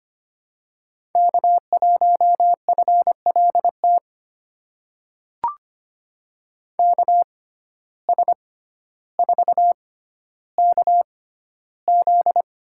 Morse code, heard as a keyed tone, about 25 wpm, spaced as for 13 wpm, 700 Hz. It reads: K1FLT KS4KZ
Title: USA Call Signs - 5 Characters - 6x Character Spacing - 25wpm